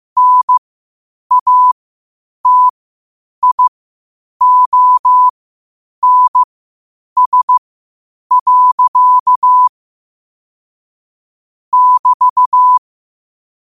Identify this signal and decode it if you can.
Morse code, heard as a keyed tone, about 15 wpm, spaced as for 8 wpm, 1000 Hz. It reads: NATIONS. <BT>
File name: qotdfiles / 8wpm